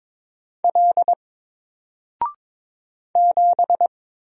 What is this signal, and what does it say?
Morse code, heard as a keyed tone, about 22 wpm, 700 Hz. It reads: L 7
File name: Single-Letter-Number-22wpm